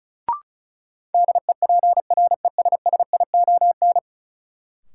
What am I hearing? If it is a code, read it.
Morse code, heard as a keyed tone, about 36 wpm, 700 Hz. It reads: DEPRESSION
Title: Sentences from Top 1300 Words - New Word Drill - 35wpm